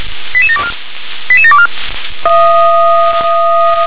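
Five-tone selective call (ZVEI), heard as quick runs of five tones, tones of about 70 ms. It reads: 88724 88724
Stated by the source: ZVEI five-tone selective call 88740 received with an SDR.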